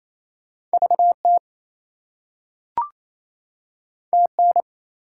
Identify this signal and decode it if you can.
Morse code, heard as a keyed tone, about 28 wpm, 700 Hz. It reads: VT TN
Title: US States and Canadian Provinces 28wpm